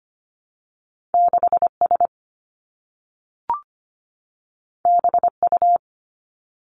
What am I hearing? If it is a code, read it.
Morse code, heard as a keyed tone, about 25 wpm, 700 Hz. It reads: 6 S B U
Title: Two Challenging Letter-Number Combinations 25wpm